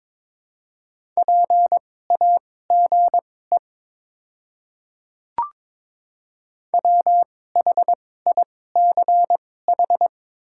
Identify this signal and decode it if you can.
Morse code, heard as a keyed tone, about 22 wpm, spaced as for 16 wpm, 700 Hz. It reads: PAGE WHICH
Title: 2x Character Spacing - Top 300 Words - 22wpm